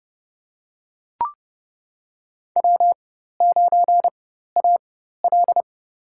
Morse code, heard as a keyed tone, about 30 wpm, spaced as for 14 wpm, 700 Hz. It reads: W9AL
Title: USA Call Signs - 4 Characters - 4x Character Spacing - 30wpm